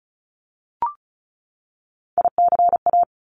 Morse code, heard as a keyed tone, about 36 wpm, 700 Hz. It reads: ICA
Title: Top Three Letter Combinations in Words 35wpm